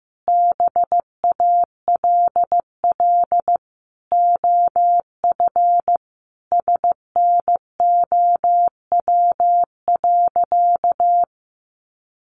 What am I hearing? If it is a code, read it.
Morse code, heard as a keyed tone, about 15 wpm, 700 Hz. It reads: BALL OF SNOW.
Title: Easy Phrases 15wpm